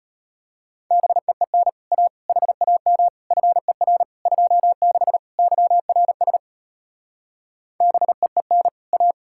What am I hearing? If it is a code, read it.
Morse code, heard as a keyed tone, about 33 wpm, 700 Hz. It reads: BEEN A HAM FER 26 YRS BEEN A